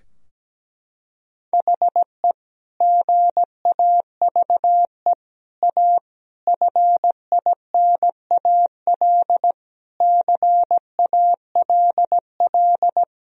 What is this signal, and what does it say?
Morse code, heard as a keyed tone, about 17 wpm, 700 Hz. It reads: HE GAVE A FINAL CALL
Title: Sentences from Top 1000 Words - Review Part 2 - 17wpm